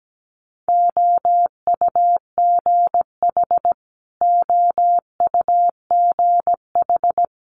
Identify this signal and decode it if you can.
Morse code, heard as a keyed tone, about 17 wpm, 700 Hz. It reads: OUGH OUGH
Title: Multi-letter Phonemes Repeated 3x 17wpm